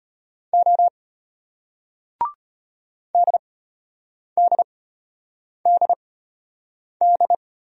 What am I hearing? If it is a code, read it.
Morse code, heard as a keyed tone, about 34 wpm, spaced as for 8 wpm, 700 Hz. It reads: O DDDD